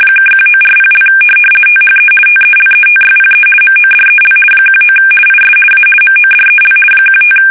RTTY, about 50 baud. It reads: (1940) THE TELETYPE MODEL 19 IS A MODEL 15
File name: fsk850-50B / teletype-corp